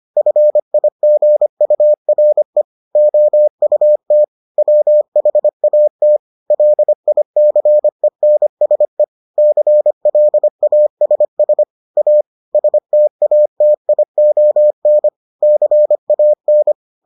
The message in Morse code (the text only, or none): FIGURE OUT WHAT LICENSE CLASS A STATION CAN